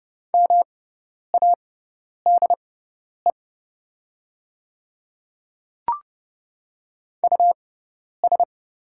Morse code, heard as a keyed tone, about 30 wpm, spaced as for 10 wpm, 700 Hz. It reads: MADE US
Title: 6x Character Spacing - Top 500 Words - 30wpm